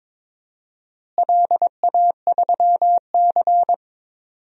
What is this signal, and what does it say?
Morse code, heard as a keyed tone, about 21 wpm, 700 Hz. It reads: LA3C
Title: DXCC-100 Call Signs - 4 Characters - Repeated 2x - 22wpm